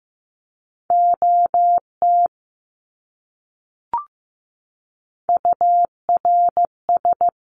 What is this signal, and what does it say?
Morse code, heard as a keyed tone, about 15 wpm, 700 Hz. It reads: OT URS
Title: Abbreviations Repeated 3x - 15wpm